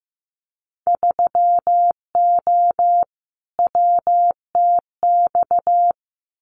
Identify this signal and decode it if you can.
Morse code, heard as a keyed tone, about 15 wpm, 700 Hz. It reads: 3O WTX
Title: Winter Field Day Exchanges Encore 15wpm